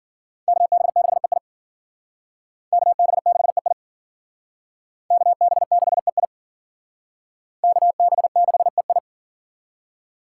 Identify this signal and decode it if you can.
Morse code, heard as a keyed tone, about 52 wpm, 700 Hz. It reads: KB6EI KB6EI KB6EI KB6EI